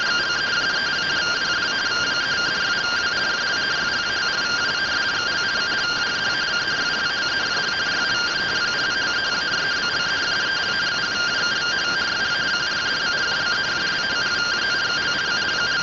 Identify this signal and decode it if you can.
Two-tone teleprinter signal, about 170 Hz shift.